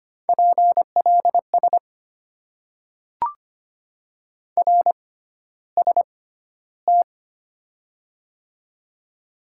Morse code, heard as a keyed tone, about 25 wpm, spaced as for 13 wpm, 690 Hz. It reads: PLS RST
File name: Top 90 QSO Elements - 6x Character Spacing - 25wpm